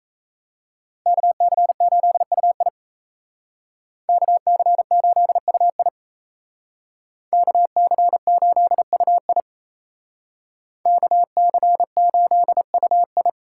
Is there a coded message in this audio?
Yes: Morse code from 1.1 to 13.3 s, heard as a keyed tone, about 36 wpm, 700 Hz. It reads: KC8UI KC8UI KC8UI KC8UI